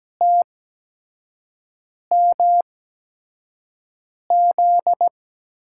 Morse code, heard as a keyed tone, about 17 wpm, spaced as for 5 wpm, 700 Hz. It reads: TMZ
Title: USA Call Signs - 6 Characters - 8x Character Spacing - 17wpm